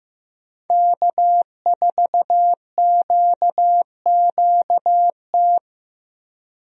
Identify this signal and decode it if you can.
Morse code, heard as a keyed tone, about 15 wpm, 700 Hz. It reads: K4QQT
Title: USA Call Signs - 5 Characters - 15wpm